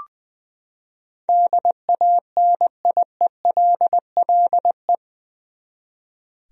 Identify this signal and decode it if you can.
Morse code, heard as a keyed tone, about 20 wpm, 700 Hz. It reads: DANIELLE
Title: USA Common Names - 20wpm